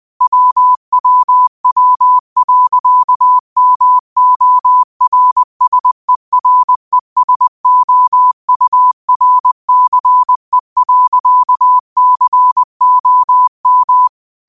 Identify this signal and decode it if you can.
Morse code, heard as a keyed tone, about 20 wpm, 1000 Hz. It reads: WWW.MORSERESOURCE.COM